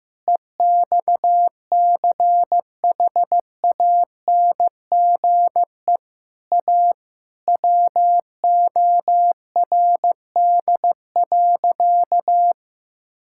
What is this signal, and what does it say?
Morse code, heard as a keyed tone, about 15 wpm, 700 Hz. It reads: EXCHANGE A WORD.
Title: Sentences from Top 1000 Words - Review Part 6 - 15wpm